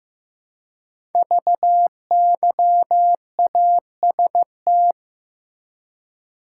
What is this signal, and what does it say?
Morse code, heard as a keyed tone, about 15 wpm, 700 Hz. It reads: VYAST
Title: Code Groups - 5 Letters - 15wpm